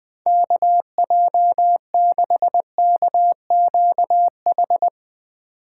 Morse code, heard as a keyed tone, about 22 wpm, 700 Hz. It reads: KJ6KQH